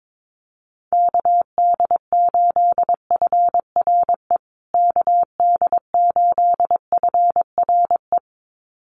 Morse code, heard as a keyed tone, about 22 wpm, 700 Hz. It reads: KD8FRE KD8FRE
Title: USA Call Signs Repeated 2x 22wpm